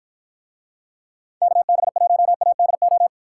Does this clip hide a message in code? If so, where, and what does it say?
Yes: Morse code from 1.4 to 3.1 s, heard as a keyed tone, about 53 wpm, 700 Hz. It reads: KB1ADO